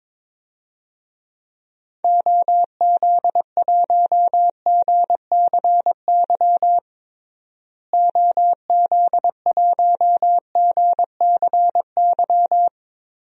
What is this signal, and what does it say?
Morse code, heard as a keyed tone, about 22 wpm, 700 Hz. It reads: OZ1GCY OZ1GCY
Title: DXCC-100 Call Signs - 6 Characters - Repeated 3x - 22wpm